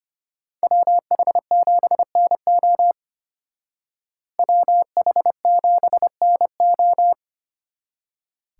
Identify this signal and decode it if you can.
Morse code, heard as a keyed tone, about 33 wpm, 700 Hz. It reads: WH7NO WH7NO